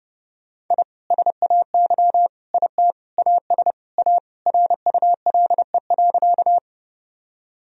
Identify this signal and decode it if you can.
Morse code, heard as a keyed tone, about 30 wpm, 700 Hz. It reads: I SAY IT AS A RULE.
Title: Sentences from Top 1000 Words - Review Part 2 - 30wpm